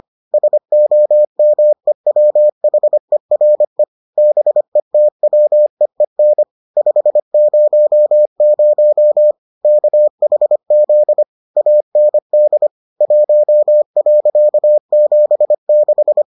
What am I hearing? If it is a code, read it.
Morse code, heard as a keyed tone, about 25 wpm, 600 Hz. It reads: SOMEWHERE BETWEEN 500 KHZ AND 1.76